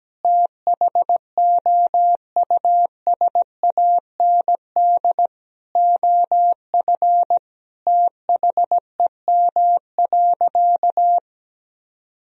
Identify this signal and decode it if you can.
Morse code, heard as a keyed tone, about 17 wpm, 700 Hz. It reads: THOUSAND OF THEM.
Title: Sentences from Top 1000 Words - Review Part 3 - 17wpm